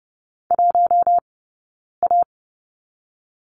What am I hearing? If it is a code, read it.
Morse code, heard as a keyed tone, about 31 wpm, 700 Hz. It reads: 1 A
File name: Lesson 16d - Letter c - Two Characters 30wpm